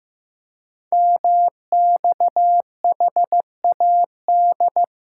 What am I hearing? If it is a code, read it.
Morse code, heard as a keyed tone, about 15 wpm, 700 Hz. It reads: MXHAD
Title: Code Groups - 5 Letters - 15wpm